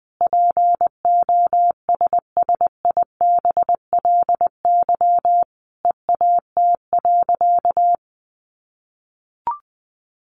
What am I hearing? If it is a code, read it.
Morse code, heard as a keyed tone, about 19 wpm, 700 Hz. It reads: POSSIBLY EAT.
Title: Sentences from Top 1300 Words 20wpm